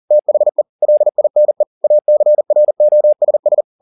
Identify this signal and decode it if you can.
Morse code, heard as a keyed tone, about 40 wpm, 600 Hz. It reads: THE LINE ACROSS
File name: the old man and the sea 2 40wpm